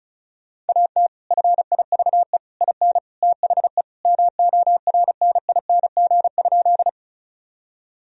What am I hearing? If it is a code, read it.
Morse code, heard as a keyed tone, about 36 wpm, 700 Hz. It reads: AT FIVE IN THE MORNING?
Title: Sentences from Top 1000 Words - Review Part 2 - 35wpm